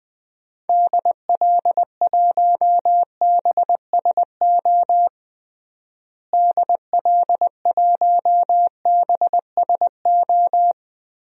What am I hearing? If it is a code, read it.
Morse code, heard as a keyed tone, about 21 wpm, 700 Hz. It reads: DL1BSO DL1BSO
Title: DXCC-100 Call Signs - 6 Characters - Repeated 2x - 20wpm